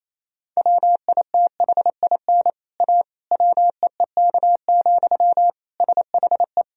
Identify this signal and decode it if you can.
Morse code, heard as a keyed tone, about 28 wpm, 700 Hz. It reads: WITHIN A WEEK, SHE
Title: Sentences from Top 1000 Words - Review Part 4 - 28wpm